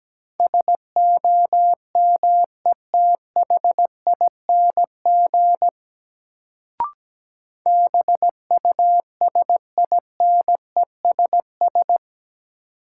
Morse code, heard as a keyed tone, about 17 wpm, 700 Hz. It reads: SOMETHING BUSINESS
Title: Sentences from Top 600 Words - New Word Drill - 17wpm